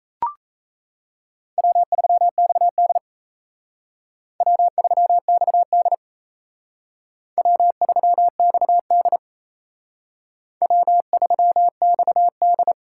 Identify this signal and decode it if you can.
Morse code, heard as a keyed tone, about 35 wpm, 700 Hz. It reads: W3XD W3XD W3XD W3XD